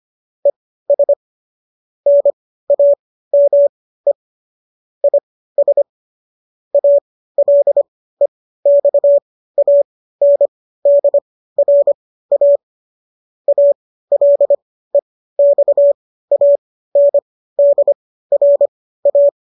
Morse code, heard as a keyed tone, about 25 wpm, spaced as for 15 wpm, 570 Hz. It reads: ES NAME IS ALEXANDRA ALEXANDRA
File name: qso219 15